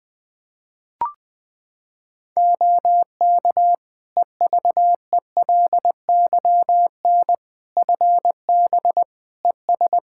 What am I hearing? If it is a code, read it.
Morse code, heard as a keyed tone, about 20 wpm, 700 Hz. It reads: OK EVELYN FB ES
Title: Practice QSO - Exchange 2 - Info - 20wpm